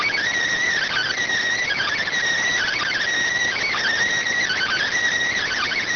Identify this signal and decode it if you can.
Emergency Alert System (SAME) tones, heard as harsh buzzing bursts. It is heard from 0.8 to 4.9 s.